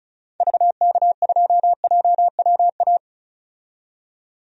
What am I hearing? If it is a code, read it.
Morse code, heard as a keyed tone, about 34 wpm, 700 Hz. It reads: VK2JWA
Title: DXCC-100 Call Signs - 6 Characters - 35wpm